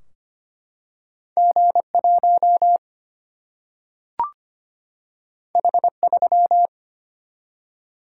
Morse code, heard as a keyed tone, about 25 wpm, 700 Hz. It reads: G1 H3
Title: Single-Letter Single-Number 25wpm